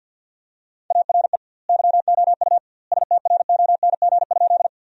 Morse code, heard as a keyed tone, about 50 wpm, 700 Hz. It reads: ARE YOU STRONG?
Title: 2x Word-Spacing - Sentences from Top 500 Words - 50wpm